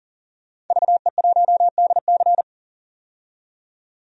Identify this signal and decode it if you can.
Morse code, heard as a keyed tone, about 40 wpm, 700 Hz. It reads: VE1DC
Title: DXCC-100 Call Signs - 5 Characters - 40wpm